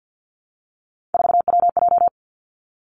Morse code, heard as a keyed tone, about 50 wpm, 700 Hz. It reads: 4WJ
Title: Three Challenging Letter-Number Combinations 50wpm